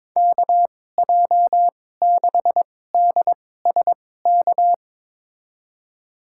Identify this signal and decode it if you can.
Morse code, heard as a keyed tone, about 22 wpm, spaced as for 16 wpm, 700 Hz. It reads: KJ6DSK
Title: USA Call Signs - 2x Character Spacing - 22wpm